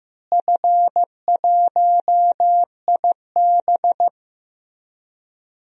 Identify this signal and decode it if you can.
Morse code, heard as a keyed tone, about 15 wpm, 700 Hz. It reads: F1IB
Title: DXCC-100 Call Signs - 4 Characters - Repeated 3x - 15wpm